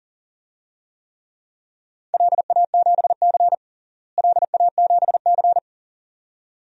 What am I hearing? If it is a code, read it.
Morse code, heard as a keyed tone, about 40 wpm, 700 Hz. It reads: LA7C LA7C
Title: DXCC-100 Call Signs - 4 Characters - Repeated 2x - 40wpm